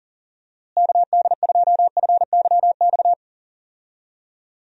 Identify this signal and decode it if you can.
Morse code, heard as a keyed tone, about 39 wpm, 700 Hz. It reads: KD2FYX
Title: USA Call Signs - 6 Characters - Repeated 2x - 40wpm